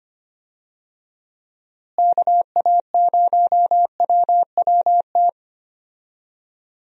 Morse code, heard as a keyed tone, about 25 wpm, 700 Hz. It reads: KA0WWT
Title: USA Call Signs - 6 Characters - Repeated 3x - 25wpm